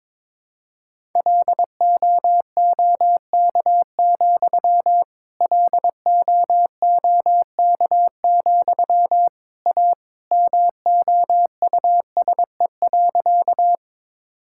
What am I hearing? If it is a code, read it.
Morse code, heard as a keyed tone, about 22 wpm, 700 Hz. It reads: LOOK, LOOK, A MOUSE.
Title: Sentences from Top 1000 Words - Review Part 5 - 22wpm